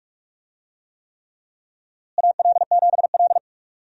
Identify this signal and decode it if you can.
Morse code, heard as a keyed tone, about 48 wpm, 700 Hz. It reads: AL7L
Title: USA Call Signs - 4 Characters - Repeated 3x - 45wpm